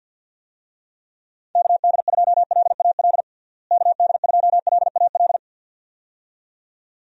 Morse code, heard as a keyed tone, about 52 wpm, 700 Hz. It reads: KD2LAL KD2LAL
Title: USA Call Signs - 6 Characters - Repeated 2x - 50wpm